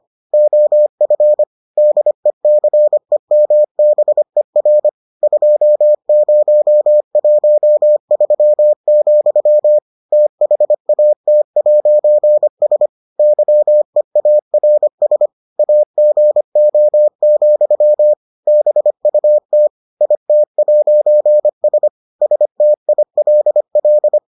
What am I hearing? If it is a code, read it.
Morse code, heard as a keyed tone, about 25 wpm, 600 Hz. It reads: OF DECEMBER 2013, THAT'S YEARS AGO, BUT IT'S STILL